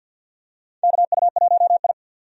Morse code, heard as a keyed tone, about 46 wpm, 700 Hz. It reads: KU1I
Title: USA Call Signs - 4 Characters - 50wpm